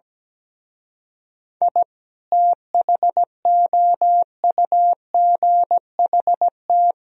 Morse code, heard as a keyed tone, about 17 wpm, 700 Hz. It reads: I THOUGHT
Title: Speed-Racing - Sets of 5 Words 17wpm